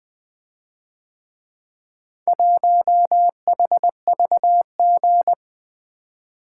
Morse code, heard as a keyed tone, about 24 wpm, 700 Hz. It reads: 1HVG
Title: Speed-Racing - Four Letter-Number Combinations - 20wpm